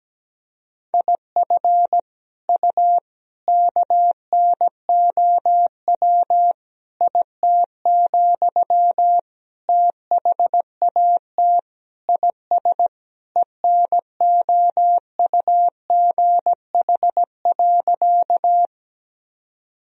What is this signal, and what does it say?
Morse code, heard as a keyed tone, about 17 wpm, 700 Hz. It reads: IF U KNOW IT, THAT IS ENOUGH.
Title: Sentences from Top 400 Words and Abbreviations 17wpm